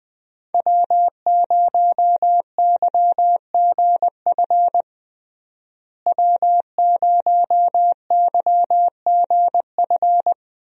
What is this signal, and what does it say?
Morse code, heard as a keyed tone, about 20 wpm, 700 Hz. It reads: W0YGF W0YGF
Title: USA Call Signs - 5 Characters - Repeated 2x - 20wpm